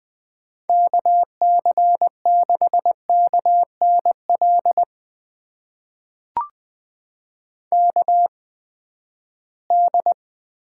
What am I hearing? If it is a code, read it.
Morse code, heard as a keyed tone, about 20 wpm, spaced as for 14 wpm, 700 Hz. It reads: KC6KNL KD
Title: USA Call Signs - 6 Characters - 8x Character Spacing - 20wpm